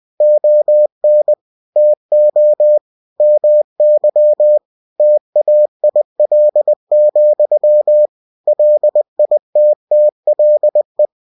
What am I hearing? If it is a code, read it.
Morse code, heard as a keyed tone, about 20 wpm, 600 Hz. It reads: ON TO MY TAIL, LITTLE
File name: winnie-20wpm